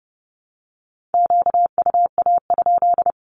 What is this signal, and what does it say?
Morse code, heard as a keyed tone, about 30 wpm, 700 Hz. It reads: QUA?
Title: Q-Codes Repeated 3x 30wpm